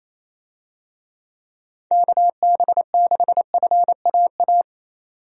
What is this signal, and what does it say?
Morse code, heard as a keyed tone, about 28 wpm, 700 Hz. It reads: KB6FAA
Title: USA Call Signs - 6 Characters - 28wpm